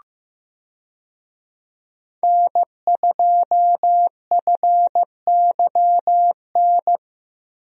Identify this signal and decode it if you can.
Morse code, heard as a keyed tone, about 15 wpm, 700 Hz. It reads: N2FYN
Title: USA Call Signs - 5 Characters - Repeated 2x - 15wpm